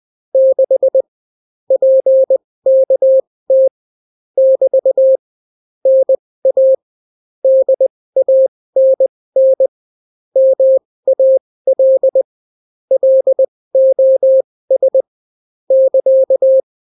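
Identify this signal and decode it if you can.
Morse code, heard as a keyed tone, about 20 wpm, spaced as for 16 wpm, 530 Hz. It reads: 6 PKT <BT> NA DANN MAL LOS <KA>
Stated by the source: Daily Morse weather and top stories at 16 wpm, clean without QRN.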